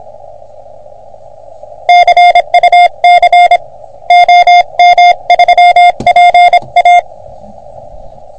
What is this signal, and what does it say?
Morse code, heard as a keyed tone, about 26 wpm, 720 Hz. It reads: CUC OM3PA
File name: OM3PA PILUP CUC 388